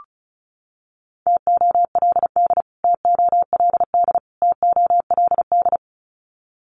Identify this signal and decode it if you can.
Morse code, heard as a keyed tone, about 34 wpm, 700 Hz. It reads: TOLD TOLD TOLD
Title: Top 500 Words Repeated 35wpm